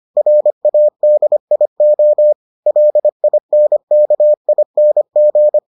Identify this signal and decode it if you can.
Morse code, heard as a keyed tone, about 25 wpm, 600 Hz. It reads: RADIO LINKING